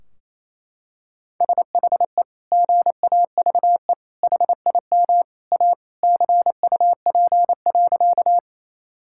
Morse code, heard as a keyed tone, about 28 wpm, 700 Hz. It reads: SHE GAVE HIM A CUP.
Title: Sentences from Top 1600 Words 28wpm